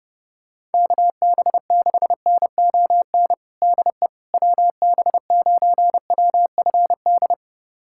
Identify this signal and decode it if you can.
Morse code, heard as a keyed tone, about 30 wpm, 700 Hz. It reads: KB6NON DE WB9WFD
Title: USA Callsign DE Callsign 30wpm